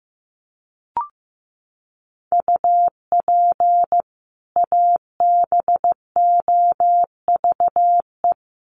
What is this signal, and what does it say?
Morse code, heard as a keyed tone, about 15 wpm, 700 Hz. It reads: UP ABOVE
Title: Sentences from Top 1200 Words 15wpm